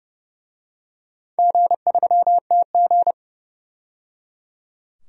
Morse code, heard as a keyed tone, about 30 wpm, 700 Hz. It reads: G3TG
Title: DXCC-100 Call Signs - 4 Characters - 30wpm